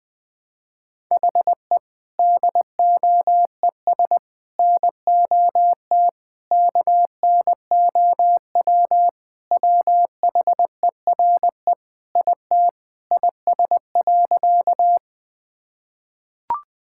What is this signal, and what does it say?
Morse code, heard as a keyed tone, about 21 wpm, 700 Hz. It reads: HE DOES NOT KNOW WHERE IT IS.
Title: Sentences with 4-Word Phrases - 20wpm